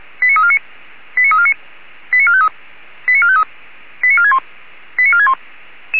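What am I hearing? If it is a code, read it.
Five-tone selective call (ZVEI), heard as quick runs of five tones, tones of about 70 ms. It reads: 89349 89349 89453 89453 89461 89461
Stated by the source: ZVEI five-tone selective call received on SDR.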